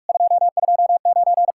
Morse code, heard as a keyed tone, about 59 wpm, spaced as for 51 wpm, 700 Hz. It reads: 229
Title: Speed-Racing Common QSO Phrases - 45wpm